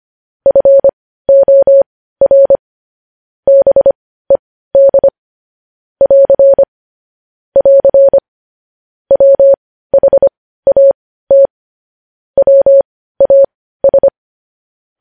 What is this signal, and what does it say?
Morse code, heard as a keyed tone, about 25 wpm, spaced as for 15 wpm, 550 Hz. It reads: FOR BED <AR> <AR> WHAT WAS
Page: news headlines in morse code at 15 wpm